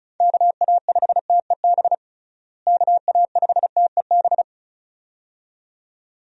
Morse code, heard as a keyed tone, about 34 wpm, 700 Hz. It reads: KA5TEB KA5TEB
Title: USA Call Signs - 6 Characters - Repeated 2x - 35wpm